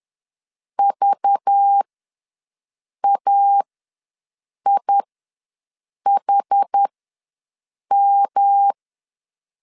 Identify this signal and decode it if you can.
Morse code, heard as a keyed tone, about 10 wpm, spaced as for 6 wpm, 800 Hz. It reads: VAIHM
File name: LEZIONI / 5 LBVC / 5 LBV ws 10wpm 3645